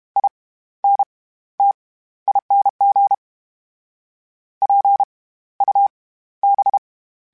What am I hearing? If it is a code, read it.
Morse code, heard as a keyed tone, about 32 wpm, spaced as for 15 wpm, 800 Hz. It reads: INTING PUB
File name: Special INGs / Special INGs-mp3s / SlowToFaster ing @30